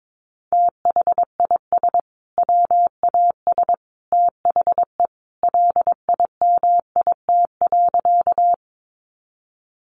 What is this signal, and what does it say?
Morse code, heard as a keyed tone, about 22 wpm, 700 Hz. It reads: THIS WAS THE LIMIT.
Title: Sentences from Top 1500 Words 22wpm